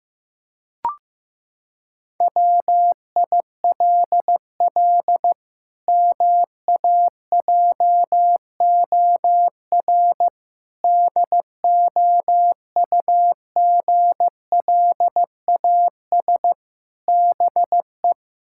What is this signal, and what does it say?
Morse code, heard as a keyed tone, about 15 wpm, 700 Hz. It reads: WILL MAJOR DOUGLAS BE
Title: Pangram Challenge v2 - 15wpm